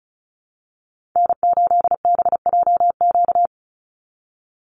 Morse code, heard as a keyed tone, about 34 wpm, 700 Hz. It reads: N8BJQ
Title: Call Signs - 35wpm